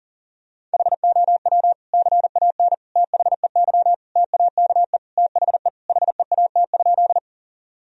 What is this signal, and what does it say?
Morse code, heard as a keyed tone, about 40 wpm, 700 Hz. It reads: HOW CAN THEY TAKE THE HEAT?
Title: Sentences from Top 1000 Words - Review Part 3 - 40wpm